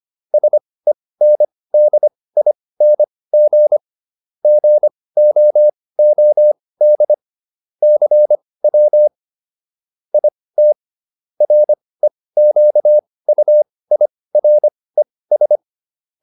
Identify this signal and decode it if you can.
Morse code, heard as a keyed tone, about 25 wpm, spaced as for 18 wpm, 600 Hz. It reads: SENDING GOOD CW IT REQUIRES